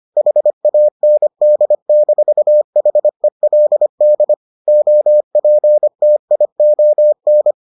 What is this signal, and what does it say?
Morse code, heard as a keyed tone, about 25 wpm, 600 Hz. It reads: HAND-HELD OPTION